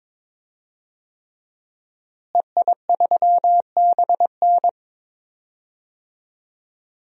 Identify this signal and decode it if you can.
Morse code, heard as a keyed tone, about 22 wpm, 700 Hz. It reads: EI3BN
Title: DXCC-100 Call Signs - 5 Characters - Repeated 3x - 22wpm